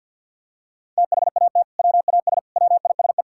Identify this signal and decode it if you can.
Morse code, heard as a keyed tone, about 50 wpm, 700 Hz. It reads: THAT WAS WISE
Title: Sentences from Top 1600 Words 50wpm